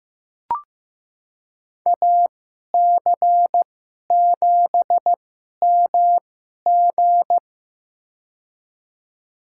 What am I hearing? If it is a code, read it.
Morse code, heard as a keyed tone, about 15 wpm, spaced as for 11 wpm, 700 Hz. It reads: AC7MG
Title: USA Call Signs - 2x Character Spacing - 15wpm